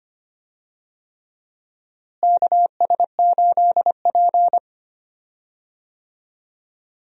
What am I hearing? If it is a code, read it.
Morse code, heard as a keyed tone, about 25 wpm, 700 Hz. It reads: KS8P
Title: USA Call Signs - 4 Characters - Repeated 3x - 25wpm